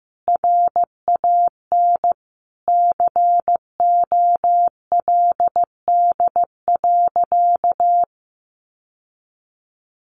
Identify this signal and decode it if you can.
Morse code, heard as a keyed tone, about 15 wpm, 700 Hz. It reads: RAN COLD.
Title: Sentences from Top 1100 Words 15wpm